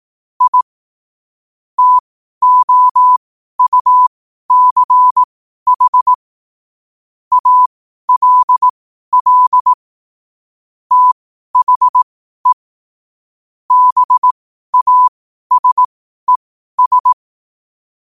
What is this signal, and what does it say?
Morse code, heard as a keyed tone, about 18 wpm, spaced as for 12 wpm, 1000 Hz. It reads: I TOUCH ALL THE BASES
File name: qotdfiles / 12wpm